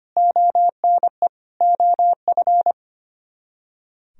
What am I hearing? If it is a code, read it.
Morse code, heard as a keyed tone, about 32 wpm, 700 Hz. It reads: ONE OF